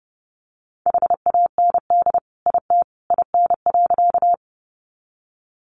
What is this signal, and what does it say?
Morse code, heard as a keyed tone, about 30 wpm, 700 Hz. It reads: HAND IT IN.
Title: Sentences from Top 200 Words 30wpm